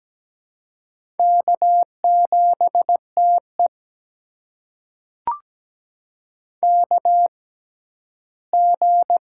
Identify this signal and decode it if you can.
Morse code, heard as a keyed tone, about 17 wpm, spaced as for 11 wpm, 700 Hz. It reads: K7TE KG
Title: USA Call Signs - 6x Character Spacing - 17wpm